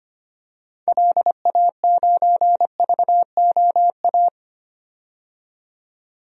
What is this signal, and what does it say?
Morse code, heard as a keyed tone, about 25 wpm, 690 Hz. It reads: LA9VOA
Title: DXCC-100 Call Signs - 6 Characters - 25wpm